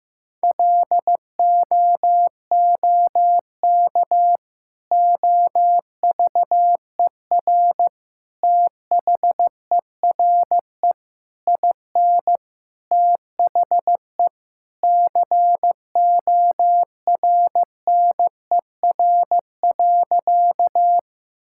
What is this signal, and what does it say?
Morse code, heard as a keyed tone, about 15 wpm, 700 Hz. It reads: LOOK OVER THERE IN THE CORNER.
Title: Sentences from Top 1600 Words 15wpm